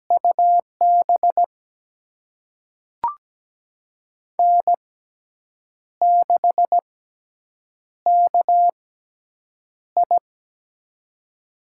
Morse code, heard as a keyed tone, about 17 wpm, spaced as for 7 wpm, 700 Hz. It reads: UB N6KI
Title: USA Call Signs - 4 Characters - 6x Character Spacing - 17wpm